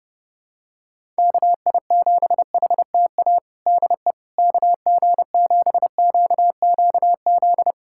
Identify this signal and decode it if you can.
Morse code, heard as a keyed tone, about 30 wpm, 700 Hz. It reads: KI7HTA DE KG7QQZ
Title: USA Callsign DE Callsign - 6 Characters - 30wpm